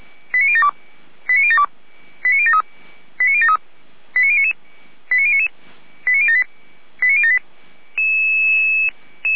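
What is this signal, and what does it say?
Five-tone selective call (ZVEI), heard as quick runs of five tones, tones of about 70 ms. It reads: 89072 89072 89073 89073 89099 89099 89078 89078
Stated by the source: Five-tone ZVEI selective call received on SDR.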